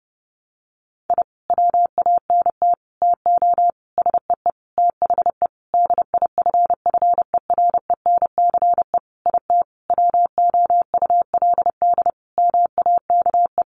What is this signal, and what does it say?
Morse code, heard as a keyed tone, about 30 wpm, 700 Hz. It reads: I WANT TO SEE THE DIFFERENCE IT WOULD MAKE
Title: Sentences from Top 1300 Words 30wpm